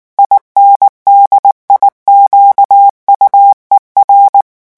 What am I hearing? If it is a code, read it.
Morse code, heard as a keyed tone, about 19 wpm, 800 Hz. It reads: INDIQUER
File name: morse mp3 / revision9 citations humoristiques vitesse 18 mots minute